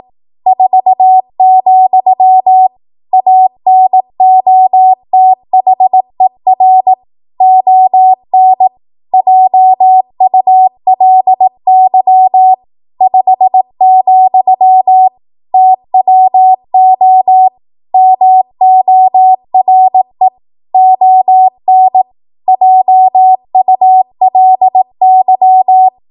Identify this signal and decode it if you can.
Morse code, heard as a keyed tone, about 18 wpm, 750 Hz. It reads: 4, ANOTHER ON JULY 5, TWO MORE ON JULY
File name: Archive / 18 WPM / 18WPM